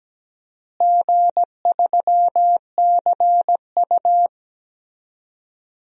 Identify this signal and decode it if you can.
Morse code, heard as a keyed tone, about 17 wpm, 700 Hz. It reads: G3CU
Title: DXCC-100 Call Signs - 4 Characters - Repeated 3x - 17wpm